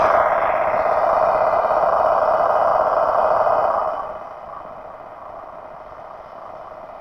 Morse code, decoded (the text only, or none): none